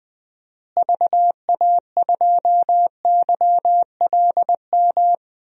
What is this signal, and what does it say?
Morse code, heard as a keyed tone, about 20 wpm, 700 Hz. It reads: VA2YLM
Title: DXCC-100 Call Signs - 6 Characters - Repeated 2x - 20wpm